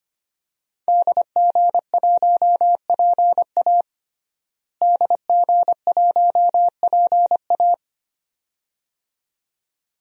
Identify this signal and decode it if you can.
Morse code, heard as a keyed tone, about 25 wpm, 700 Hz. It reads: DG1PA DG1PA
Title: DXCC-100 Call Signs - 5 Characters - Repeated 2x - 25wpm